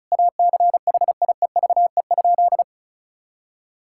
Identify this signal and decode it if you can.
Morse code, heard as a keyed tone, about 36 wpm, 700 Hz. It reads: ACHIEVE?
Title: Sentences from Top 1000 Words - Review Part 6 - 35wpm